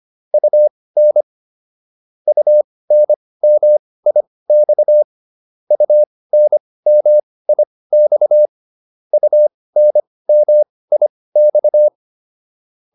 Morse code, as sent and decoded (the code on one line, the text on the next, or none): ..- -. / ..- -. -- .. -..- / ..- -. -- .. -..- / ..- -. -- .. -..-
UN UNMIX UNMIX UNMIX